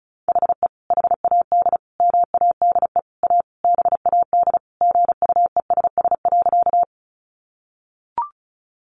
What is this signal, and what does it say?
Morse code, heard as a keyed tone, about 34 wpm, 700 Hz. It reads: HE HAD MADE A BAD GUESS.
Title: Sentences from Top 1500 Words 35wpm